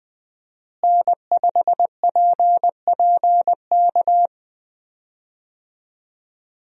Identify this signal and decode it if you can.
Morse code, heard as a keyed tone, about 20 wpm, 700 Hz. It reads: N5PPK
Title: USA Call Signs - 5 Characters - Repeated 3x - 20wpm